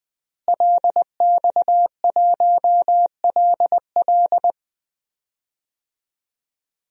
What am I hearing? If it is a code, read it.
Morse code, heard as a keyed tone, about 20 wpm, 700 Hz. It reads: LX1LL
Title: DXCC-100 Call Signs - 5 Characters - Repeated 2x - 20wpm